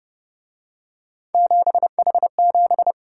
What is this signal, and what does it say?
Morse code, heard as a keyed tone, about 31 wpm, 700 Hz. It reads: 7H7